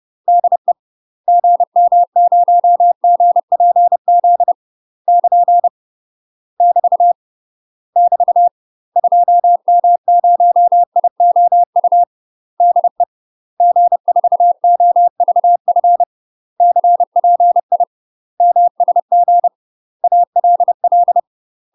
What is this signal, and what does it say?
Morse code, heard as a keyed tone, about 30 wpm, 700 Hz. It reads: DE GM0GPZ <KN> <BT> <BT> 2M0IOU DE G4OVF CPI MSG ALL
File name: UK MASTER QSOs TRAINING/30WPM